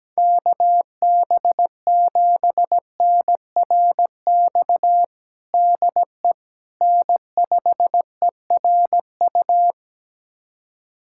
Morse code, heard as a keyed tone, about 17 wpm, 700 Hz. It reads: KB7NRX DE N5ERU
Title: USA Callsign DE Callsign 17wpm